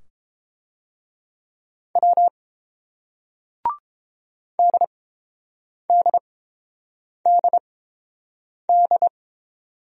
Morse code, heard as a keyed tone, about 31 wpm, spaced as for 7 wpm, 700 Hz. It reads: W DDDD